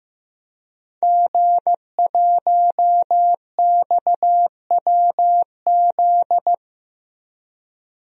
Morse code, heard as a keyed tone, about 15 wpm, 700 Hz. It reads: G1XWZ
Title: DXCC-100 Call Signs - 5 Characters - Repeated 3x - 15wpm